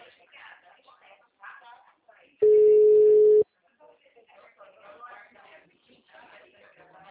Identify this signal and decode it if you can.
Ringback tone (Europe, 425 Hz), with speech.